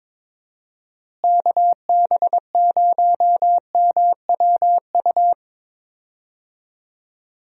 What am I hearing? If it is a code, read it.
Morse code, heard as a keyed tone, about 22 wpm, 700 Hz. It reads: KB0MWU
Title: USA Call Signs - 6 Characters - Repeated 3x - 22wpm